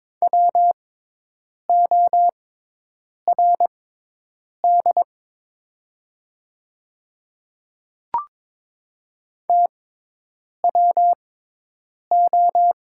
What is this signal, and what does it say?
Morse code, heard as a keyed tone, about 21 wpm, spaced as for 8 wpm, 700 Hz. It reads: WORD TWO
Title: 6x Character Spacing - Top 100 Words - 22wpm